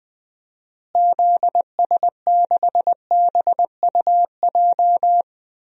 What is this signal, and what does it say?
Morse code, heard as a keyed tone, about 20 wpm, 700 Hz. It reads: ZS6BUJ
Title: DXCC-100 Call Signs - 6 Characters - Repeated 3x - 20wpm